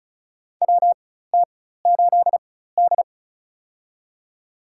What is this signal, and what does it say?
Morse code, heard as a keyed tone, about 36 wpm, spaced as for 16 wpm, 700 Hz. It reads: WT8D
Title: USA Call Signs - 4 Characters - 4x Character Spacing - 35wpm